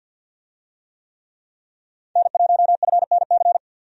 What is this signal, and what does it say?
Morse code, heard as a keyed tone, about 51 wpm, 700 Hz. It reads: N1FNC
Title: USA Call Signs - 5 Characters - 50wpm